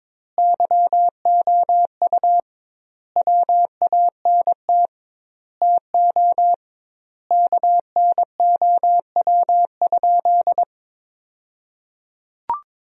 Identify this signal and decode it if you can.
Morse code, heard as a keyed tone, about 22 wpm, 700 Hz. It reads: YOU WANT TO KNOW?
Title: 2x Word-Spacing - Sentences from Top 200 Words - 22wpm